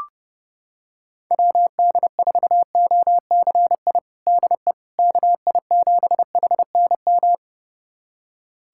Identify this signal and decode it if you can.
Morse code, heard as a keyed tone, about 30 wpm, 700 Hz. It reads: WD4OCI DE KI7HNM
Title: USA Callsign DE Callsign - 6 Characters - 30wpm